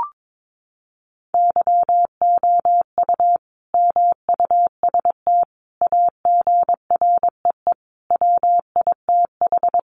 Morse code, heard as a keyed tone, about 22 wpm, 700 Hz. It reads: YOU MUST AGREE WITH
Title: Sentences from Top 1100 Words 22wpm